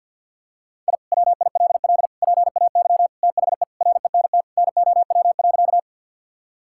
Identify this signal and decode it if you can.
Morse code, heard as a keyed tone, about 47 wpm, 700 Hz. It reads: I WILL PAY THE RENT NOW.
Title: Sentences from Top 1000 Words - Review Part 5 - 50wpm